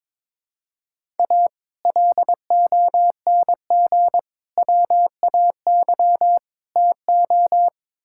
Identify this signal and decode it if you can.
Morse code, heard as a keyed tone, about 22 wpm, 700 Hz. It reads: A LONG WAY TO
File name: Build A Phrase - 5-Word Phrases - 22wpm